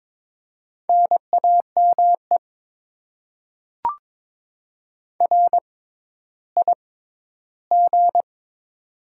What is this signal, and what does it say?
Morse code, heard as a keyed tone, about 22 wpm, spaced as for 12 wpm, 700 Hz. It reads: NAME RIG
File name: Top 20 QSO Elements - 6x Character Spacing - 22wpm